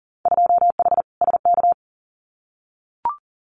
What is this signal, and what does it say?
Morse code, heard as a keyed tone, about 40 wpm, 700 Hz. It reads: 2H SK
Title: Winter Field Day Exchanges Encore 40wpm